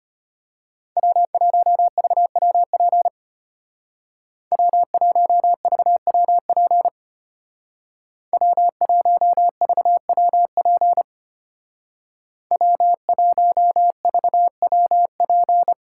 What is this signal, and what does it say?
Morse code, heard as a keyed tone, about 31 wpm, 700 Hz. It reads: W1VWP W1VWP W1VWP W1VWP
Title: Speed Racing - USA 1x3 Call Signs 25wpm